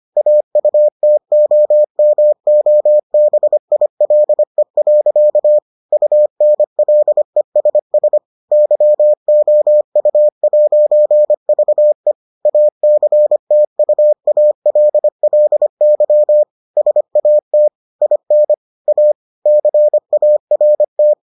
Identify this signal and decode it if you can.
Morse code, heard as a keyed tone, about 25 wpm, 600 Hz. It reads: AUTOMOBILE. UNLESS YOU'VE ACTUALLY SAT IN A CART